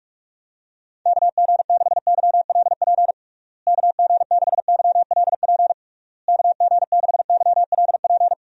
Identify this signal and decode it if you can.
Morse code, heard as a keyed tone, about 46 wpm, 700 Hz. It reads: KG6YLP KG6YLP KG6YLP
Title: USA Call Signs - 6 Characters - Repeated 3x - 45wpm